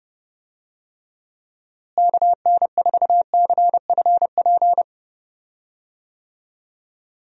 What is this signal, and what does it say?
Morse code, heard as a keyed tone, about 31 wpm, 700 Hz. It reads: KN4CFP
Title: USA Call Signs - 6 Characters - Repeated 3x - 30wpm